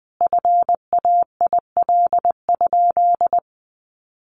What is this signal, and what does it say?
Morse code, heard as a keyed tone, about 20 wpm, 700 Hz. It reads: FAIL?
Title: Sentences from Top 1100 Words 20wpm